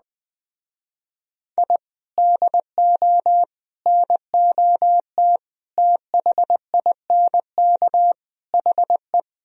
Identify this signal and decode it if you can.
Morse code, heard as a keyed tone, about 20 wpm, 700 Hz. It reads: I DO NOT THINK HE